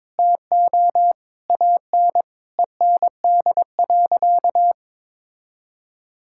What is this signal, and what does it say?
Morse code, heard as a keyed tone, about 22 wpm, 700 Hz. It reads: TO AN END.
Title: Sentences from Top 1000 Words - Review Part 5 - 22wpm